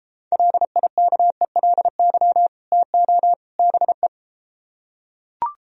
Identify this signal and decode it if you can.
Morse code, heard as a keyed tone, about 33 wpm, 700 Hz. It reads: LIKELY TO BE